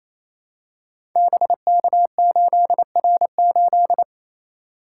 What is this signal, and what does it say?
Morse code, heard as a keyed tone, about 28 wpm, 700 Hz. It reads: BK8R8
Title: Code Groups - 5 Letter:Numbers - 28wpm